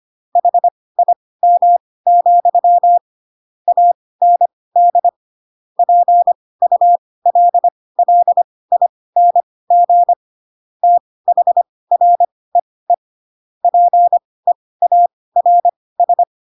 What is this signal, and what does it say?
Morse code, heard as a keyed tone, about 25 wpm, spaced as for 18 wpm, 700 Hz. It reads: HIM, AND PULLING THREE PEARS
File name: Pinocchio / Pinocchio ch7 12 25 ik1hge